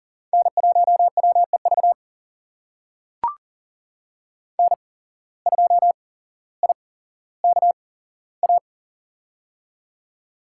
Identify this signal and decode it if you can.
Morse code, heard as a keyed tone, about 39 wpm, spaced as for 17 wpm, 700 Hz. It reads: N1WEV N2IKA
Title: USA Call Signs - 5 Characters - 8x Character Spacing - 40wpm